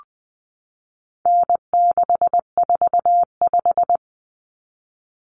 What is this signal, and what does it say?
Morse code, heard as a keyed tone, about 20 wpm, 700 Hz. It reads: N645
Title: Common Cut Numbers - 4-Digit 20wpm